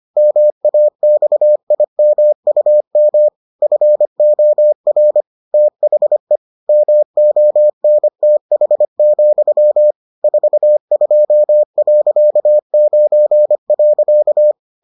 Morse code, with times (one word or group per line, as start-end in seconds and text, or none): 0.2-3.3 MAXIMUM
3.6-5.2 FOR
5.5-6.4 THE
6.7-9.9 MONTH,
10.2-14.5 42.9.